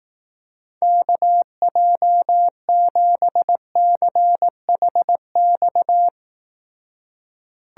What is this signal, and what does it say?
Morse code, heard as a keyed tone, about 18 wpm, 700 Hz. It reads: KJ7CHX
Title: Speed Racing - USA 2x3 Call Signs 15wpm